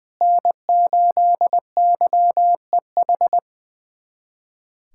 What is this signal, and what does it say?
Morse code, heard as a keyed tone, about 21 wpm, 700 Hz. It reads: N8YEH
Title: USA Callsign DE Callsign - 5 Characters - 20wpm